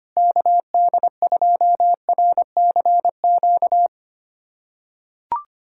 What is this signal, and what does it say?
Morse code, heard as a keyed tone, about 25 wpm, 700 Hz. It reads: KD2RCQ
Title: USA Callsign DE Callsign - 6 Characters - 25wpm